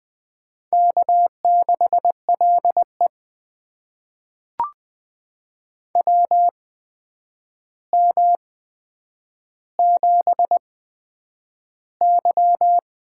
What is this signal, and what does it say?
Morse code, heard as a keyed tone, about 20 wpm, spaced as for 9 wpm, 700 Hz. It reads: K6LE WM7Y
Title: USA Call Signs - 4 Characters - 8x Character Spacing - 20wpm